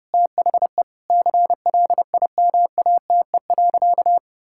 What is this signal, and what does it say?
Morse code, heard as a keyed tone, about 32 wpm, 700 Hz. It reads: THE CLIMATE.
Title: Sentences from Top 1600 Words 30wpm